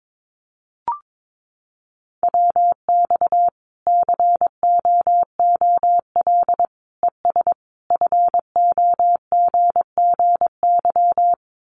Morse code, heard as a keyed tone, about 22 wpm, 700 Hz. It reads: WX COOL ES FOGGY
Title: Common QSO Phrases 22wpm